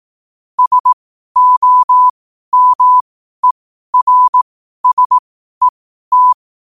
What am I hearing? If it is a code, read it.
Morse code, heard as a keyed tone, about 18 wpm, spaced as for 12 wpm, 1000 Hz. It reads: SOMERSET
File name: qotdfiles / 12wpm